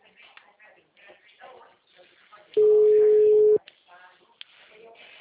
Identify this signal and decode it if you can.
Ringback tone (Europe, 425 Hz), with speech.
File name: out- -2019- - - .6